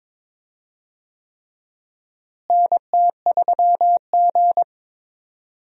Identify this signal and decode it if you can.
Morse code, heard as a keyed tone, about 21 wpm, 700 Hz. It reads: NT3G
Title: USA Call Signs - 4 Characters - Repeated 3x - 22wpm